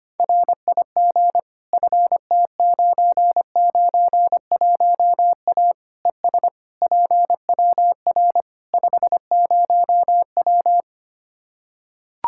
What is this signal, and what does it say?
Morse code, heard as a keyed tone, about 25 wpm, 700 Hz. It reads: RIG FT991A ES PWR 50W
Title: K1USN SST QSO Practice 25wpm